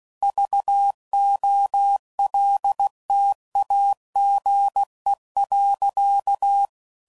Morse code, heard as a keyed tone, about 16 wpm, 800 Hz. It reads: VOLTAGE.
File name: CW Code Groups @15WPM / Technical Subjects / XL